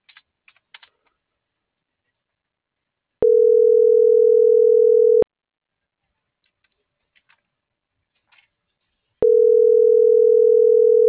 Ringback tone (North America), 440 + 480 Hz.